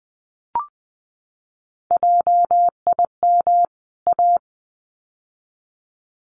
Morse code, heard as a keyed tone, about 20 wpm, 700 Hz. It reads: JIM A
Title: CWT Exchange 20wpm